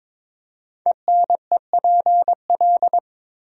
Morse code, heard as a keyed tone, about 22 wpm, 700 Hz. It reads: ENEPL
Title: Code Groups - 5 Letters - 22wpm